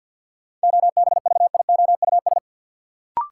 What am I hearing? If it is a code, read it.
Morse code, heard as a keyed tone, about 53 wpm, 700 Hz. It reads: OBVIOUS
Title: Sentences from Top 1600 Words - New Word Drill Alt - 50wpm